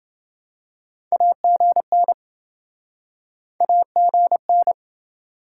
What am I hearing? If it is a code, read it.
Morse code, heard as a keyed tone, about 28 wpm, 700 Hz. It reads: AGN AGN